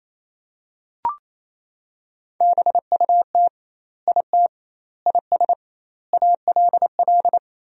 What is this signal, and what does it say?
Morse code, heard as a keyed tone, about 28 wpm, 700 Hz. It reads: BUT IT IS ALL
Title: 2x Word-Spacing - Sentences from Top 300 Words - 28wpm